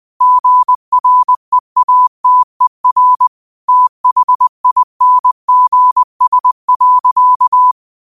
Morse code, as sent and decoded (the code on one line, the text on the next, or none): --. .-. .- - . .-. / - .... .. -. --. ... .-.-.-
GREATER THINGS.